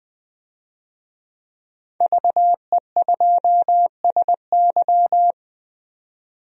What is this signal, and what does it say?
Morse code, heard as a keyed tone, about 20 wpm, 700 Hz. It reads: VE2SY
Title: DXCC-100 Call Signs - 5 Characters - 20wpm